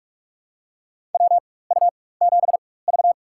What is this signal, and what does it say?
Morse code, heard as a keyed tone, about 44 wpm, spaced as for 21 wpm, 700 Hz. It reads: WU7V
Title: USA Call Signs - 4 Characters - 4x Character Spacing - 45wpm